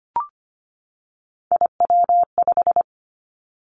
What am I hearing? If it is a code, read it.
Morse code, heard as a keyed tone, about 25 wpm, 700 Hz. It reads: IW5
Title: Three Challenging Letter-Number Combinations 25wpm